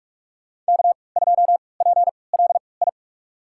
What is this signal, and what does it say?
Morse code, heard as a keyed tone, about 46 wpm, spaced as for 25 wpm, 700 Hz. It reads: K2PLI
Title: USA Call Signs - 5 Characters - 3x Character Spacing - 45wpm